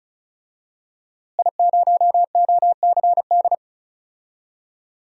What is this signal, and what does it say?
Morse code, heard as a keyed tone, about 34 wpm, 700 Hz. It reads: I0OCD
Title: DXCC-100 Call Signs - 5 Characters - Repeated 2x - 35wpm